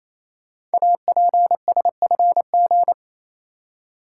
Morse code, heard as a keyed tone, about 27 wpm, 700 Hz. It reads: APSFG
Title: Code Groups - 5 Letters - 28wpm